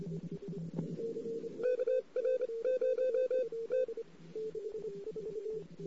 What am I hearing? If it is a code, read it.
Morse code, heard as a keyed tone, about 29 wpm, spaced as for 22 wpm, 510 Hz. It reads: KR0T
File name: CW / 2008-ss-cw-k5tr-op-n5rz -14:00